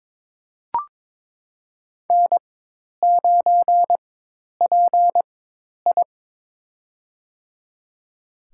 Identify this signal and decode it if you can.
Morse code, heard as a keyed tone, about 22 wpm, spaced as for 10 wpm, 700 Hz. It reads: N9PI
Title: USA Call Signs - 4x Character Spacing - 22wpm